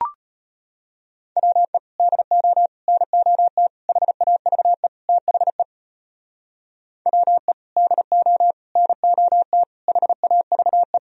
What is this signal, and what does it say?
Morse code, heard as a keyed tone, about 33 wpm, 700 Hz. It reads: WE DO NOT HAVE THE WE DO NOT HAVE